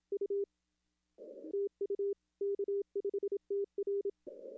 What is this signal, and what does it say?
Morse code, heard as a keyed tone, about 26 wpm, 410 Hz. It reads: U TU K5TR
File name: contests / arrl-160-k5tr-op-w5tn -0300